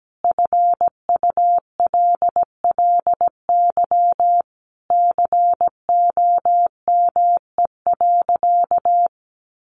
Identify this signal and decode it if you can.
Morse code, heard as a keyed tone, about 17 wpm, 700 Hz. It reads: FULLY COME.
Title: Sentences from Top 1300 Words 17wpm